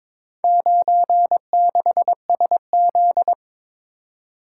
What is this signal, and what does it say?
Morse code, heard as a keyed tone, about 23 wpm, 700 Hz. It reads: 96SZ
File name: practice / 22wpm / Four Letter-Number Combinations - 22wpm